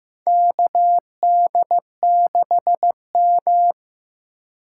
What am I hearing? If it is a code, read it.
Morse code, heard as a keyed tone, about 15 wpm, 700 Hz. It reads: KD6M
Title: USA Call Signs - 4 Characters - 3x Character Spacing - 15wpm